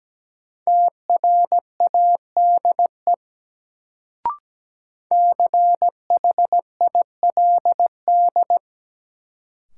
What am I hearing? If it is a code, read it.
Morse code, heard as a keyed tone, about 17 wpm, 700 Hz. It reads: TRADE CHILD
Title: Sets of 5 Words Encore - New Word Drill - 17wpm